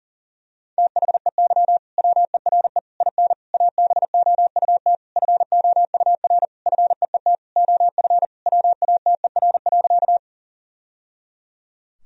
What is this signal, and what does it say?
Morse code, heard as a keyed tone, about 40 wpm, 700 Hz. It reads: THEY WERE IN ABOUT FOUR FEET OF WATER.
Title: Sentences from Top 1000 Words - Review Part 1 - 40wpm